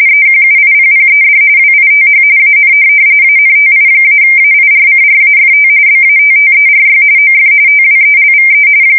RTTY, about 45.45 baud. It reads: EQUIPPED WITH AN ANTI-AIRCRAFT BATTERY OF TWELVE